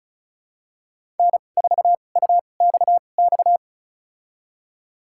Morse code, heard as a keyed tone, about 34 wpm, spaced as for 25 wpm, 700 Hz. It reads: N4UXX
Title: USA Call Signs - 5 Characters - 2x Character Spacing - 35wpm